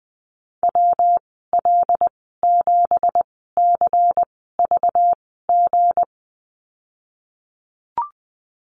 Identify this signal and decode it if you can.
Morse code, heard as a keyed tone, about 20 wpm, spaced as for 15 wpm, 700 Hz. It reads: WL7CVG
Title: 193 Lesson 31g - Letter v - Call Signs 20wpm